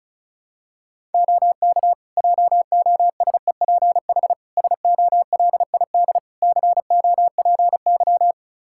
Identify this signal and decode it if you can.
Morse code, heard as a keyed tone, about 34 wpm, 700 Hz. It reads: OK JOSEPH SOLID COPY
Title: Practice QSO - Exchange 3 - More Info - 35wpm